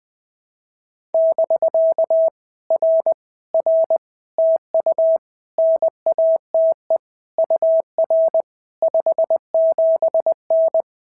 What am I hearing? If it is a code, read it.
Morse code, heard as a keyed tone, about 20 wpm, 650 Hz. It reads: <BK> R R TU NATE UR 57N
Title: POTA QSO Practice - 20wpm